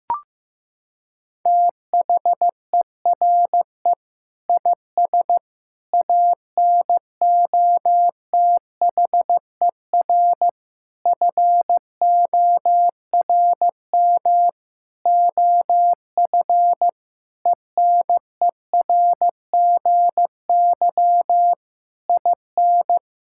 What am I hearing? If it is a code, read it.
Morse code, heard as a keyed tone, about 16 wpm, 700 Hz. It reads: THERE IS ANOTHER FORM OF ENERGY IN
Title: Sentences from Top 1000 Words - Review Part 6 - 15wpm